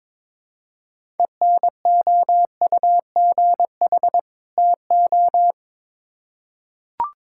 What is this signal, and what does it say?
Morse code, heard as a keyed tone, about 22 wpm, 700 Hz. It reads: ENOUGH TO
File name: Build A Phrase - 2-Word Phrases - 22wpm